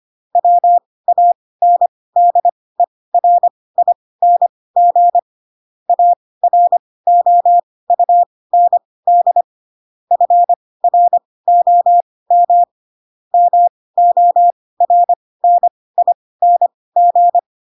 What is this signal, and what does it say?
Morse code, heard as a keyed tone, about 25 wpm, spaced as for 18 wpm, 700 Hz. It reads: WANDERING AROUND FROM MORNING
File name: Pinocchio / Pinocchio ch4 12 25 ik1hge